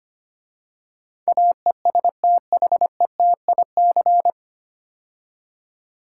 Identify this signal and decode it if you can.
Morse code, heard as a keyed tone, about 25 wpm, 700 Hz. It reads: AESTHETIC
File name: practice / 25wpm / 9 Letter Words - First 3 Letters Given - 25wpm